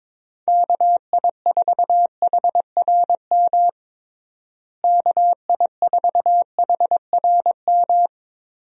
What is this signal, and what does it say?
Morse code, heard as a keyed tone, about 22 wpm, 700 Hz. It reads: KI4HRM KI4HRM
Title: USA Call Signs - 6 Characters - Repeated 3x - 22wpm